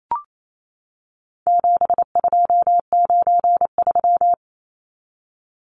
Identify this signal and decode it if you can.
Morse code, heard as a keyed tone, about 28 wpm, 700 Hz. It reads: 7293
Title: Numbers - 4-Digit 28wpm